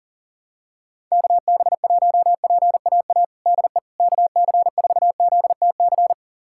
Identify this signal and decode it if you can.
Morse code, heard as a keyed tone, about 40 wpm, 700 Hz. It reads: KB1PAA DE KC4ZTC
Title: USA Callsign DE Callsign - 6 Characters - 40wpm